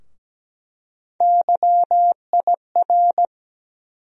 Morse code, heard as a keyed tone, about 17 wpm, 700 Hz. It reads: YIR
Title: USA Call Sign Postfixes - 3 Letters 17wpm